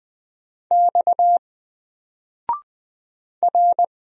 Morse code, heard as a keyed tone, about 20 wpm, 700 Hz. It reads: X R
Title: Single Letter-Number 20wpm